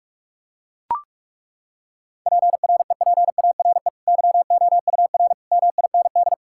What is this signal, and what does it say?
Morse code, heard as a keyed tone, about 41 wpm, 700 Hz. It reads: PREPARE YOUR MIND
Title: Sentences from Top 1000 Words - Review Part 6 - 45wpm